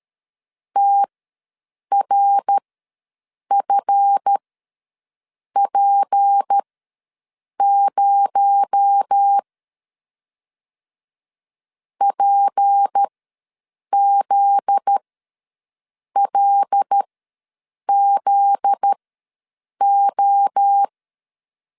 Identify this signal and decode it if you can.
Morse code, heard as a keyed tone, about 13 wpm, spaced as for 7 wpm, 800 Hz. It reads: TRFP0 PZLZO
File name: LEZIONI / 7 GF9Z / 7 GF9Z ws 12wpm 3321